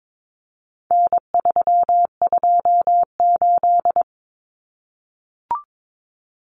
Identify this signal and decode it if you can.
Morse code, heard as a keyed tone, about 22 wpm, 700 Hz. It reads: N328
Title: Common Cut Numbers - 4-Digit 22wpm